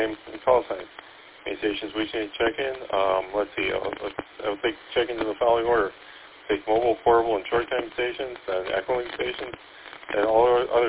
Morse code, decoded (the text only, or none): none